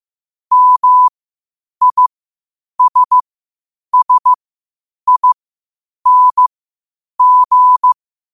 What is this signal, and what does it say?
Morse code, heard as a keyed tone, about 15 wpm, spaced as for 8 wpm, 1000 Hz. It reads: MISSING
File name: qotdfiles / 8wpm